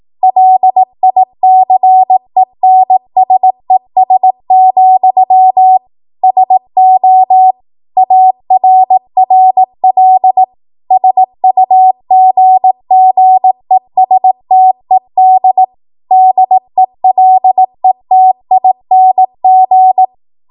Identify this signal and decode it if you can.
Morse code, heard as a keyed tone, about 18 wpm, 750 Hz. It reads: LICENSES, SO ARRL SUGGESTED DELETING
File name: Morse / 18WPM